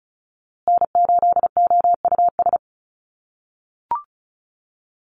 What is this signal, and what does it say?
Morse code, heard as a keyed tone, about 34 wpm, 700 Hz. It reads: N8OUS
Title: USA Call Signs Repeated 3x 35wpm